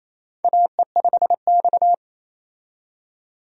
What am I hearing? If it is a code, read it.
Morse code, heard as a keyed tone, about 28 wpm, 700 Hz. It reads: AE5X
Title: USA Call Signs - 4 Characters - 28wpm